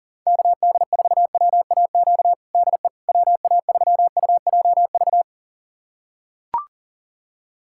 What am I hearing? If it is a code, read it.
Morse code, heard as a keyed tone, about 40 wpm, 700 Hz. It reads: KD4WAQ DE WA3UJV
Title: USA Callsign DE Callsign 40wpm